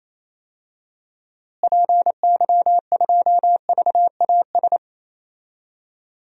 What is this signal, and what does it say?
Morse code, heard as a keyed tone, about 28 wpm, 700 Hz. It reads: PY2VAS
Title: DXCC-100 Call Signs - 6 Characters - Repeated 2x - 28wpm